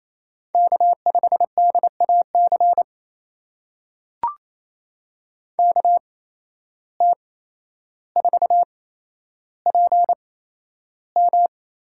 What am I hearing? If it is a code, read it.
Morse code, heard as a keyed tone, about 28 wpm, spaced as for 12 wpm, 700 Hz. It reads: K5DAC KT4PM
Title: USA Call Signs - 5 Characters - 8x Character Spacing - 28wpm